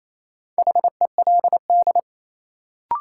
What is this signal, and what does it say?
Morse code, heard as a keyed tone, about 28 wpm, 700 Hz. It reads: HELD
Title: Sentences from Top 1600 Words - New Word Drill - 28wpm